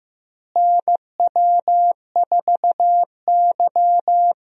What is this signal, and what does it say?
Morse code, heard as a keyed tone, about 15 wpm, 700 Hz. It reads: NW4Y
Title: USA Call Signs - 4 Characters - Repeated 2x - 15wpm